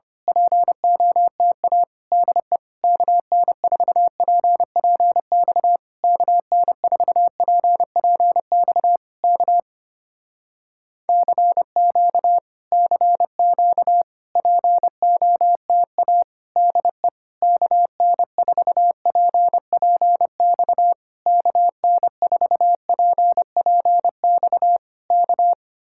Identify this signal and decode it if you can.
Morse code, heard as a keyed tone, about 25 wpm, 700 Hz. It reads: POTA DE KN4PPX KN4PPX K CQ CQ POTA DE KN4PPX KN4PPX K
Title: Speed-Racing Common QSO Phrases - 25wpm